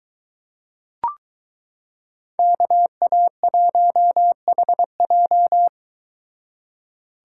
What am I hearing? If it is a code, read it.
Morse code, heard as a keyed tone, about 23 wpm, 700 Hz. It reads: KA1HJ